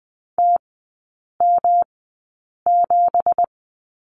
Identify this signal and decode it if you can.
Morse code, heard as a keyed tone, about 20 wpm, spaced as for 8 wpm, 700 Hz. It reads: TM7
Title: 166 Lesson 27e - Number 7 - Three Characters 20wpm